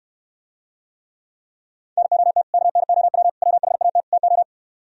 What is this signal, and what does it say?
Morse code, heard as a keyed tone, about 67 wpm, 700 Hz. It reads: NOT KNOW WHAT TO